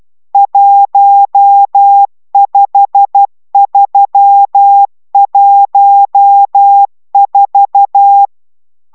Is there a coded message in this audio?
Yes: Morse code from 0.3 to 8.3 s, heard as a keyed tone, about 12 wpm, 800 Hz. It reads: 15314